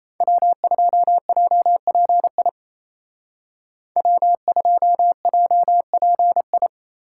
Morse code, heard as a keyed tone, about 34 wpm, 700 Hz. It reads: W2JPI W2JPI
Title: Speed Racing - USA 1x3 Call Signs 28wpm